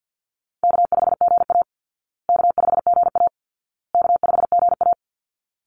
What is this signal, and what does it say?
Morse code, heard as a keyed tone, about 50 wpm, 700 Hz. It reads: K5GA K5GA K5GA
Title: Easy Call Signs Repeated 3x - 50wpm